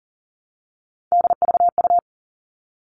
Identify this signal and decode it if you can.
Morse code, heard as a keyed tone, about 40 wpm, 700 Hz. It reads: DVU
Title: All Cut Numbers - 3-Digit 40wpm